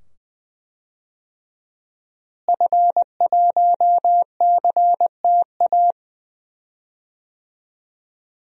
Morse code, heard as a keyed tone, about 20 wpm, 700 Hz. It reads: F1CTA
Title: DXCC-100 Call Signs - 5 Characters - Repeated 2x - 20wpm